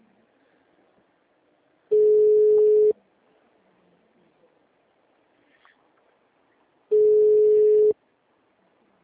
Ringback tone (Europe, 425 Hz).
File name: out- -2010- - - .20